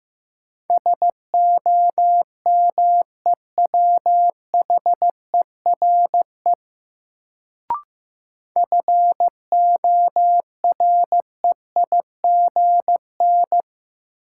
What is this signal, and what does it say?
Morse code, heard as a keyed tone, about 15 wpm, 700 Hz. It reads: SOMEWHERE FOREIGN
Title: Sentences from Top 1400 Words - New Word Drill - 15wpm